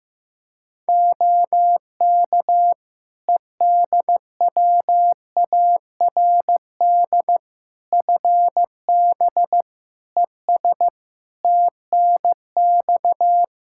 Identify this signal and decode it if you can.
Morse code, heard as a keyed tone, about 15 wpm, 700 Hz. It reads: OK EDWARD FB ES TNX
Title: Practice QSO - Exchange 2 - Info - 15wpm